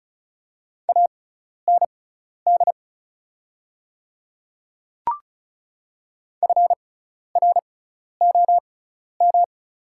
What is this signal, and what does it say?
Morse code, heard as a keyed tone, about 36 wpm, spaced as for 12 wpm, 700 Hz. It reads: AND FROM
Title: 6x Character Spacing - Top 300 Words - 35wpm